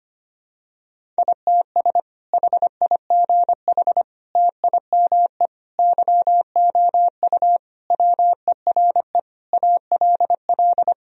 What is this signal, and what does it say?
Morse code, heard as a keyed tone, about 25 wpm, 700 Hz. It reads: ITS HIGH TIME YOU WERE ALL
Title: Sentences from Top 1000 Words - Review Part 1 - 25wpm